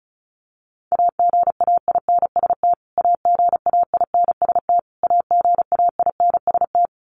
Morse code, heard as a keyed tone, about 34 wpm, 700 Hz. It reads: AGAINST AGAINST AGAINST
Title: Top 500 Words Repeated 35wpm